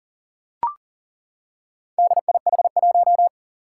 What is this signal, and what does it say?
Morse code, heard as a keyed tone, about 40 wpm, 700 Hz. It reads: DIH1
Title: Four Challenging Letter-Number Combinations 40wpm